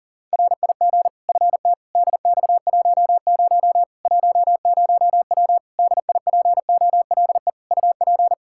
Morse code, heard as a keyed tone, about 40 wpm, 700 Hz. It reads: RIG FT DX10 10W DIPOLE UP
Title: Practice QSO - Exchange 2 - Info - 40wpm